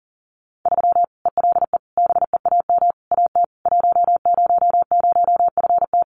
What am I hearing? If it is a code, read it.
Morse code, heard as a keyed tone, about 40 wpm, 700 Hz. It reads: 3 ELE BEAM AT 100FT
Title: Common QSO Phrases 40wpm